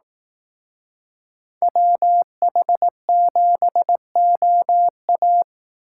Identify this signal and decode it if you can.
Morse code, heard as a keyed tone, about 18 wpm, 700 Hz. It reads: WH7OA
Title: Speed Racing - USA 2x2 Call Signs 15wpm